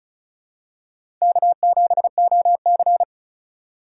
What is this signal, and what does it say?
Morse code, heard as a keyed tone, about 36 wpm, 690 Hz. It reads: K7OC
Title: USA Call Signs - 4 Characters - 2x Character Spacing - 35wpm